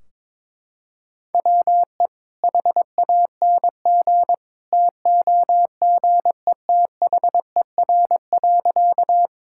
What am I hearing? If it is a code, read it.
Morse code, heard as a keyed tone, about 22 wpm, 700 Hz. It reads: WE HANG TOGETHER.
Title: Sentences from Top 1600 Words 22wpm